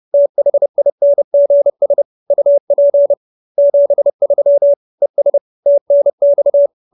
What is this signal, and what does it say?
Morse code, heard as a keyed tone, about 30 wpm, 570 Hz. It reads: THINGS UP 73 ES TNX
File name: QSO321 30